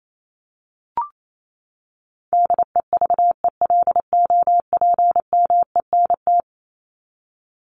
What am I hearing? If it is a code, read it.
Morse code, heard as a keyed tone, about 28 wpm, 700 Hz. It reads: DEVELOPMENT
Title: Top 501-750 Words 28wpm